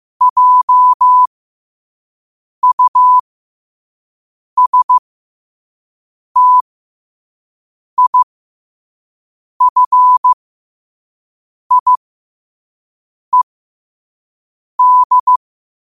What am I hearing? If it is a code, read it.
Morse code, heard as a keyed tone, about 15 wpm, spaced as for 5 wpm, 1000 Hz. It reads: JUSTIFIED
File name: 5wpm